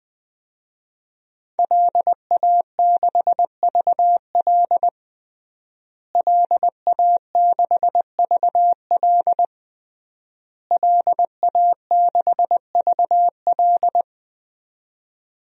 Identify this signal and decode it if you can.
Morse code, heard as a keyed tone, about 20 wpm, 700 Hz. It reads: LA6VL LA6VL LA6VL
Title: DXCC-100 Call Signs - 5 Characters - Repeated 3x - 20wpm